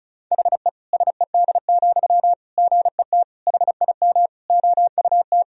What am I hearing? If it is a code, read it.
Morse code, heard as a keyed tone, about 34 wpm, 700 Hz. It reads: HE SED, GET HIM OUT
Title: Sentences from Top 400 Words and Abbreviations 35wpm